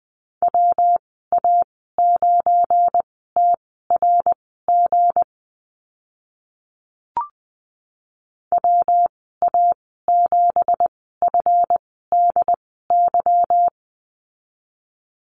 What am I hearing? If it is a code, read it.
Morse code, heard as a keyed tone, about 20 wpm, spaced as for 15 wpm, 700 Hz. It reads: WA9TRG WA7FDY
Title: 186 Lesson 30g - Letter b - Call Signs 20wpm